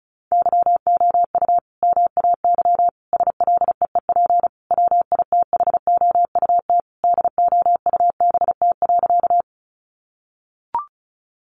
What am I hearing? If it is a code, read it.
Morse code, heard as a keyed tone, about 34 wpm, 700 Hz. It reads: YOU MAY SLEEP WITHOUT DOUBT.
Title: Sentences from Top 1500 Words 35wpm